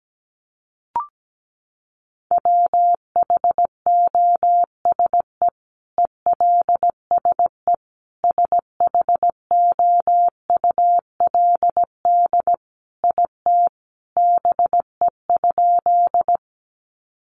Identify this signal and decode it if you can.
Morse code, heard as a keyed tone, about 17 wpm, 700 Hz. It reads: WHOSE ELSE SHOULD IT BE?
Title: Sentences from Top 1200 Words 17wpm